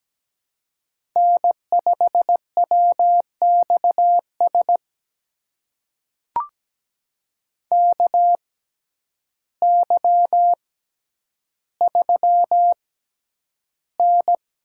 Morse code, heard as a keyed tone, about 17 wpm, spaced as for 9 wpm, 700 Hz. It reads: N5WXS KY3N
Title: USA Call Signs - 6x Character Spacing - 17wpm